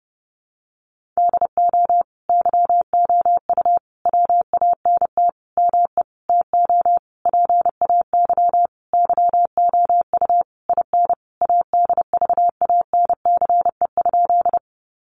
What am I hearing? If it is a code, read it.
Morse code, heard as a keyed tone, about 30 wpm, 700 Hz. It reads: DO YOU WANT ME TO PAY YOU IN ADVANCE?
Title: Sentences from Top 1400 Words 30wpm